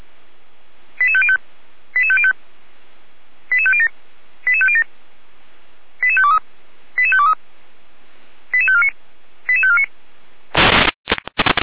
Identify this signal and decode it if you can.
Five-tone selective call (ZVEI), heard as quick runs of five tones, tones of about 70 ms. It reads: 88505 88505 88507 88507 88523 88523 88549 88549